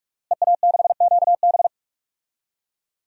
Morse code, heard as a keyed tone, about 45 wpm, 700 Hz. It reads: EA6QB
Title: DXCC-100 Call Signs - 5 Characters - Repeated 2x - 45wpm